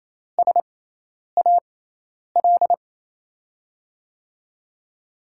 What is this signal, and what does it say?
Morse code, heard as a keyed tone, about 28 wpm, spaced as for 10 wpm, 700 Hz. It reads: SAL
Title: USA Call Signs - 6x Character Spacing - 28wpm